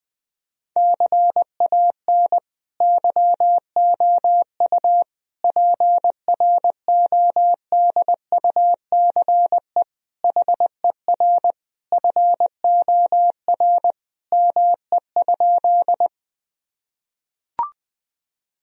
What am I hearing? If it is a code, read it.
Morse code, heard as a keyed tone, about 20 wpm, 700 Hz. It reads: CAN YOU PRODUCE HER FOR ME?
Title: Sentences from Top 1000 Words - Review Part 3 - 20wpm